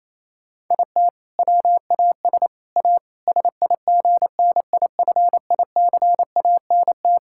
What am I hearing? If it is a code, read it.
Morse code, heard as a keyed tone, about 28 wpm, 700 Hz. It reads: IT WAS A SIGNIFICANT
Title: Sentences from Top 1000 Words - Review Part 6 - 28wpm